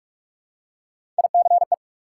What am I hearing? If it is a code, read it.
Morse code, heard as a keyed tone, about 46 wpm, 700 Hz. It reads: ICE
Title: Three Letter Words - 45wpm